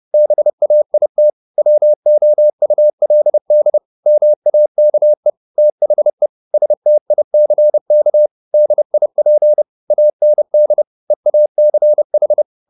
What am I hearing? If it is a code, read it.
Morse code, heard as a keyed tone, about 30 wpm, 600 Hz. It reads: BAIT WOULD MAKE THE STICK DIP AND EACH
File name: the old man and the sea / the old man and the sea 1 30wpm